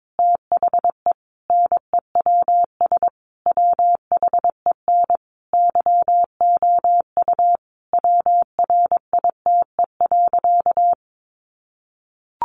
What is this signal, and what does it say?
Morse code, heard as a keyed tone, about 22 wpm, 700 Hz. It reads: THE NEWS WHEN YOU WRITE.
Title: Sentences from Top 800 Words 22wpm